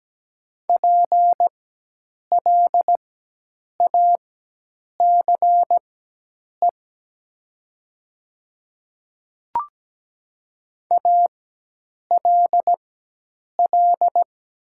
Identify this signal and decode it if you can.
Morse code, heard as a keyed tone, about 17 wpm, spaced as for 8 wpm, 700 Hz. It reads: PLACE ALL
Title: 4x Character Spacing - Top 200 Words - 17wpm